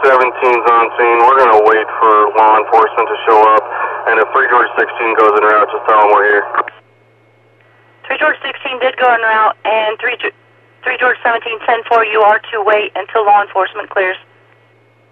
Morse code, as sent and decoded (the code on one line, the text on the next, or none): none